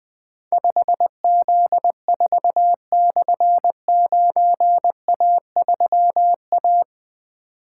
Morse code, heard as a keyed tone, about 20 wpm, 700 Hz. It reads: 5Z4/9A3A
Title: Difficult Call Signs 20wpm